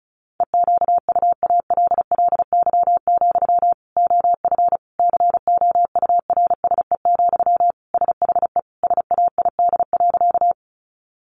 Morse code, heard as a keyed tone, about 34 wpm, 700 Hz. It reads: EQUALLY, OF COURSE, SHE SAID.
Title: Sentences from Top 1400 Words 35wpm